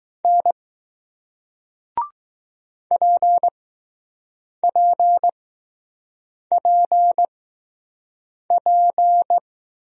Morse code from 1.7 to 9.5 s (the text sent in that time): PPPP